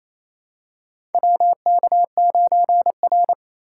Morse code, heard as a keyed tone, about 28 wpm, 700 Hz. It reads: WK9R
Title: USA Call Signs - 4 Characters - 4x Character Spacing - 28wpm